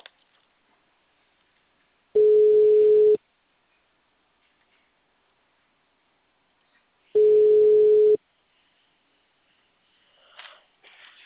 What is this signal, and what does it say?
Ringback tone (Europe, 425 Hz).
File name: out- -2008- - - .97